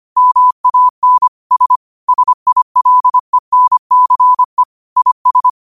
Morse code, heard as a keyed tone, about 25 wpm, 1000 Hz. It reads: MAN S SILENCE IS
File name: qotdfiles / 25wpm